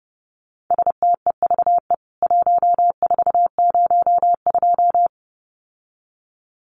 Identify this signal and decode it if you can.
Morse code, heard as a keyed tone, about 30 wpm, 700 Hz. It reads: STEVE 1402
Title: CWT Exchange 30wpm